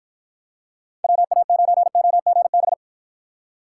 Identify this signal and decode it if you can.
Morse code, heard as a keyed tone, about 52 wpm, 700 Hz. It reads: WA9OGB